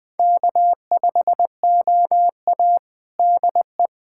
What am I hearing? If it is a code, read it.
Morse code, heard as a keyed tone, about 20 wpm, 700 Hz. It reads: K5OA DE
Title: USA Callsign DE Callsign - 4 Characters - 20wpm